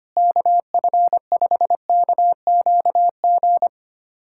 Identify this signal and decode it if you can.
Morse code, heard as a keyed tone, about 25 wpm, 700 Hz. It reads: KF5KQG
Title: USA Callsign DE Callsign - 6 Characters - 25wpm